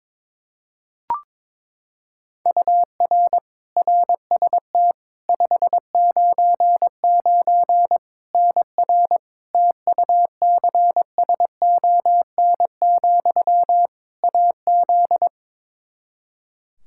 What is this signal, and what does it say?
Morse code, heard as a keyed tone, about 22 wpm, 700 Hz. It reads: UR RST 599 NR TUCSON, AZ
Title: Practice QSO - Exchange 1 - Beginning - 22wpm